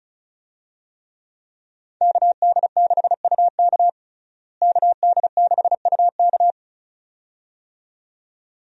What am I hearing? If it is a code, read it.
Morse code, heard as a keyed tone, about 35 wpm, 700 Hz. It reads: KD6UK KD6UK
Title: USA Call Signs - 5 Characters - Repeated 2x - 35wpm